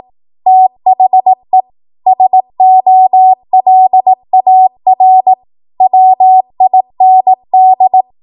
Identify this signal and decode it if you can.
Morse code, heard as a keyed tone, about 18 wpm, 750 Hz. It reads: THE SOLAR WIND
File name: Morse / 18WPM